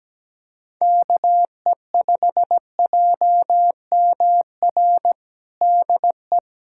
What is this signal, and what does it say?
Morse code, heard as a keyed tone, about 17 wpm, 700 Hz. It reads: KE5JMR DE
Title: USA Callsign DE Callsign 17wpm